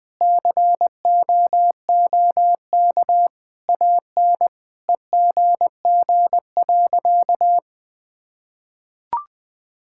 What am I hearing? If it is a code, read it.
Morse code, heard as a keyed tone, about 21 wpm, 700 Hz. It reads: COOK AN EGG.
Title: Sentences with 7-Word Phrases - Spoken Beginning - 20wpm